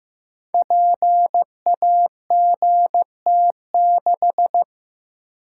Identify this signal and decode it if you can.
Morse code, heard as a keyed tone, about 15 wpm, 700 Hz. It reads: PAGT6
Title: Code Groups - 5 Letter:Numbers - 15wpm